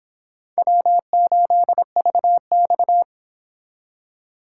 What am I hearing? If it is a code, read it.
Morse code, heard as a keyed tone, about 26 wpm, 700 Hz. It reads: W8VX